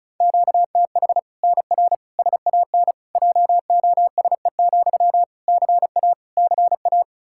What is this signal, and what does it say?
Morse code, heard as a keyed tone, about 34 wpm, 700 Hz. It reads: QTH NR SAN JOSE, CA CA
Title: Practice QSO - Exchange 1 - Beginning - 35wpm